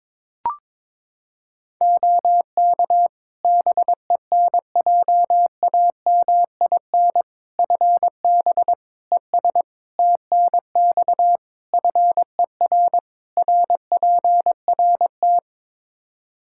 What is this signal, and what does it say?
Morse code, heard as a keyed tone, about 22 wpm, 700 Hz. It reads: OK BENJAMIN FB ES TNX FER RPRT
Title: K1USN SST QSO Practice 22wpm